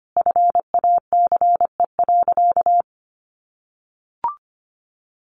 Morse code, heard as a keyed tone, about 25 wpm, 700 Hz. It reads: FACE.
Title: Sentences from Top 700 Words 25wpm